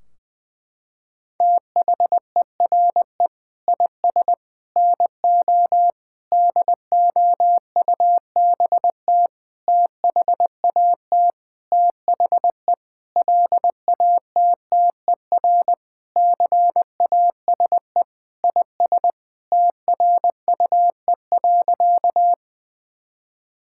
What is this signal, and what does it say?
Morse code, heard as a keyed tone, about 21 wpm, 700 Hz. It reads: THERE IS NO DOUBT THAT THE LATTER CASE IS TRUE.
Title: Sentences from Top 1600 Words 20wpm